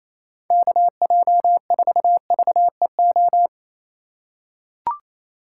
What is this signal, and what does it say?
Morse code, heard as a keyed tone, about 28 wpm, 700 Hz. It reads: KJ4VEO
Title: USA Call Signs - 6 Characters - 4x Character Spacing - 28wpm